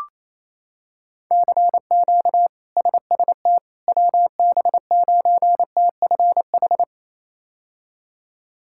Morse code, heard as a keyed tone, about 28 wpm, 700 Hz. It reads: CQ SST WB9TFH
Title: K1USN SST QSO Practice - 28wpm